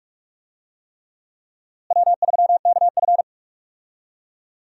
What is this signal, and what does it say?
Morse code, heard as a keyed tone, about 46 wpm, 700 Hz. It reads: W3KF
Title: USA Call Signs - 4 Characters - Repeated 3x - 45wpm